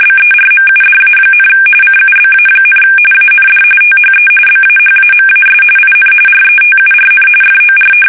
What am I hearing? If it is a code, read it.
RTTY, about 45.45 baud. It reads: BERMUDA, CUBA, AND THE BAHAMAS IN RETURN FOR
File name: fsk850 / usnavy3